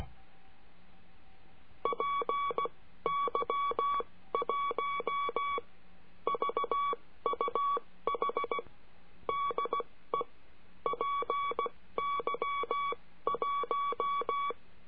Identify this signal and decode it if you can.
Morse code, heard as a keyed tone, about 16 wpm, spaced as for 13 wpm, 1130 Hz. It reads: PY1VUH DE PY1
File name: cursos / CURSO DE TELEGRAFIA / Telegrafia - Trilha 22